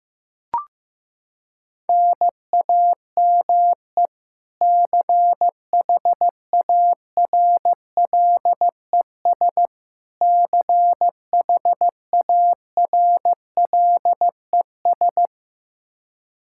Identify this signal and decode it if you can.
Morse code, heard as a keyed tone, about 15 wpm, 700 Hz. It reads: NAME CHARLES CHARLES
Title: Practice QSO - Exchange 1 - Beginning - 15wpm